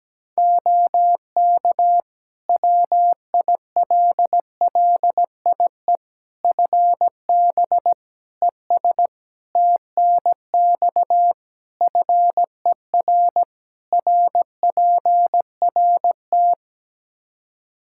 Morse code, heard as a keyed tone, about 17 wpm, 700 Hz. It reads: OK WILLIE FB ES TNX FER RPRT
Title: Practice QSO - Exchange 2 - Info - 17wpm